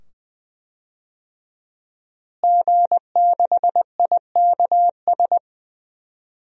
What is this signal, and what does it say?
Morse code, heard as a keyed tone, about 20 wpm, 700 Hz. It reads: G6IKS
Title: DXCC-100 Call Signs - 5 Characters - Repeated 2x - 20wpm